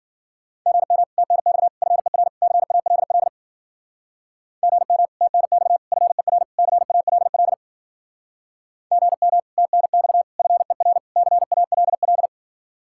Morse code, heard as a keyed tone, about 50 wpm, 700 Hz. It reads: GM TNX FER CALL GM TNX FER CALL GM TNX FER CALL
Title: Speed-Racing Common QSO Phrases Encore - 40wpm